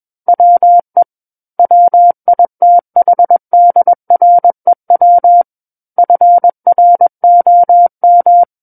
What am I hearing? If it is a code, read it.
Morse code, heard as a keyed tone, about 21 wpm, 700 Hz. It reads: WE WITHDREW FROM
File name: Book One / 12-Chapter 12 - 21 WPM